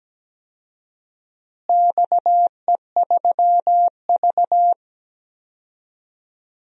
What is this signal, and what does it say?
Morse code, heard as a keyed tone, about 17 wpm, 700 Hz. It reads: XE3V
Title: DXCC-100 Call Signs - 4 Characters - Repeated 2x - 17wpm